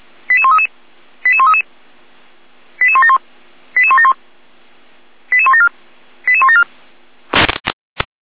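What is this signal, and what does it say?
Five-tone selective call (ZVEI), heard as quick runs of five tones, tones of about 70 ms. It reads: 88133 88133 88171 88171 88174 88174
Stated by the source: ZVEI five-tone selective call, received on an SDR.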